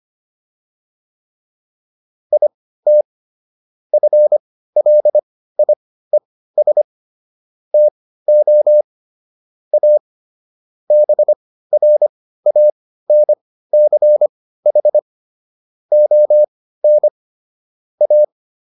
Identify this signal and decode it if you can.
Morse code, heard as a keyed tone, about 25 wpm, spaced as for 15 wpm, 600 Hz. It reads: IT FLIES TO A BRANCH ON A